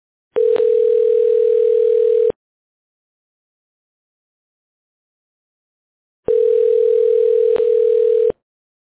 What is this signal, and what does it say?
Ringback tone (North America), 440 + 480 Hz.